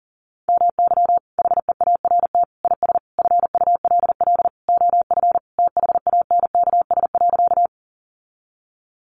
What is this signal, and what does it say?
Morse code, heard as a keyed tone, about 40 wpm, 700 Hz. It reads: MY HEART IS FULL OF THANKS.
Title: Sentences from Top 1200 Words 40wpm